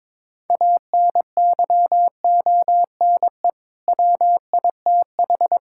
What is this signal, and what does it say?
Morse code, heard as a keyed tone, about 22 wpm, 700 Hz. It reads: ANYONE WITH
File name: practice / 22wpm / Sets of 6 Words - 22wpm